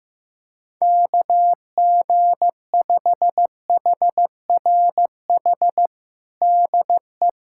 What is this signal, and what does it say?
Morse code, heard as a keyed tone, about 15 wpm, 700 Hz. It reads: KG5HRH DE
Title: USA Callsign DE Callsign - 6 Characters - 15wpm